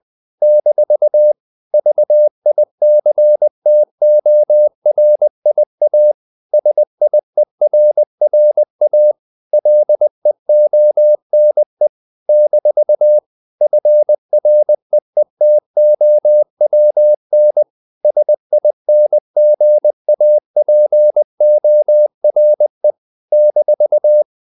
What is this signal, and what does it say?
Morse code, heard as a keyed tone, about 20 wpm, 600 Hz. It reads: - VICTORIA SIERRA LEONE - FREETOWN SINGAPORE -
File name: CW / Practice / NationalCapitals20